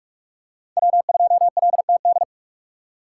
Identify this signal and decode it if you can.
Morse code, heard as a keyed tone, about 46 wpm, 700 Hz. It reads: W2LTD
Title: USA Call Signs - 5 Characters - 6x Character Spacing - 45wpm